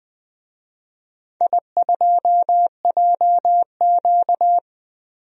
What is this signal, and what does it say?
Morse code, heard as a keyed tone, about 20 wpm, 700 Hz. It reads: I2JQ
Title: DXCC-100 Call Signs - 4 Characters - 20wpm